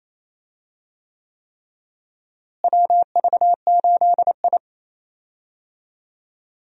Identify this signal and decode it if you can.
Morse code, heard as a keyed tone, about 28 wpm, 700 Hz. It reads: WV8I
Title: USA Call Signs - 4 Characters - Repeated 3x - 28wpm